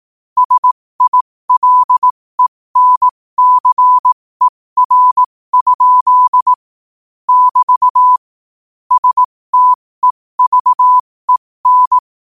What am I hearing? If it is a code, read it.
Morse code, heard as a keyed tone, about 18 wpm, spaced as for 15 wpm, 1000 Hz. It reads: SILENCER? <BT> STEVEN
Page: New quotes every day in morse code at 15 Words per minute.